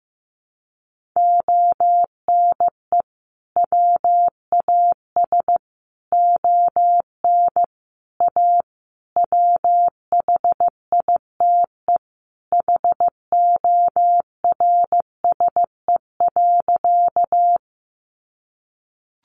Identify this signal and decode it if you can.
Morse code, heard as a keyed tone, about 15 wpm, 700 Hz. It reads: ONE WAS ON A WHITE HORSE.
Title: Sentences from Top 300 Words 15wpm